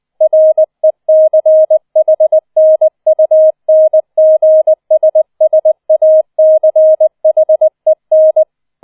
Morse code, heard as a keyed tone, about 20 wpm, 630 Hz. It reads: RECHNUNGSSACHEN
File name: 004 Vierter Streich 20wpm